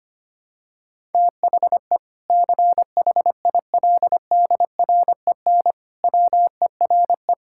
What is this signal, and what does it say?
Morse code, heard as a keyed tone, about 25 wpm, 700 Hz. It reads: THE CHILDREN WERE
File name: practice / 25wpm / Difficult Sentences with Hints - 25wpm